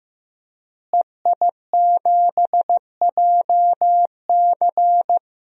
Morse code, heard as a keyed tone, about 15 wpm, 700 Hz. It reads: EI7JC
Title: DXCC-100 Call Signs - 5 Characters - Repeated 2x - 15wpm